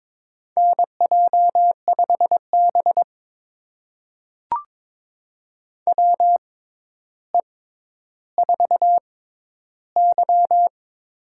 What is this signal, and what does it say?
Morse code, heard as a keyed tone, about 22 wpm, spaced as for 11 wpm, 700 Hz. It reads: NJ5B WE4Y
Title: USA Call Signs - 4 Characters - 6x Character Spacing - 22wpm